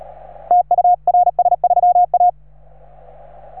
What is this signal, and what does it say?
Morse code, heard as a keyed tone, about 37 wpm, 720 Hz. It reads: TU RS3A